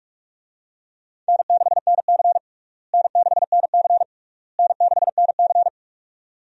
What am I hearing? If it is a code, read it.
Morse code, heard as a keyed tone, about 46 wpm, 700 Hz. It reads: N6NC N6NC N6NC
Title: USA Call Signs - 4 Characters - Repeated 3x - 45wpm